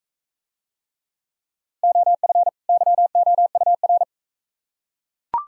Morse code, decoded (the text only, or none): OF YOUR